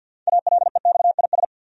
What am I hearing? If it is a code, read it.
Morse code, heard as a keyed tone, about 52 wpm, 700 Hz. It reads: ALEXIS
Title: USA Common Names - 50wpm